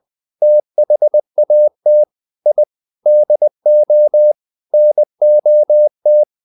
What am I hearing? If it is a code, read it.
Morse code, heard as a keyed tone, about 20 wpm, 600 Hz. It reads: THAT I DO NOT
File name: the old man and the sea / the old man and the sea part 1 20wpm